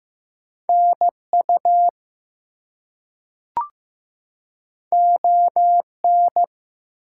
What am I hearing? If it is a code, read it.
Morse code, heard as a keyed tone, about 15 wpm, 700 Hz. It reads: NU ON
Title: US States and Canadian Provinces 15wpm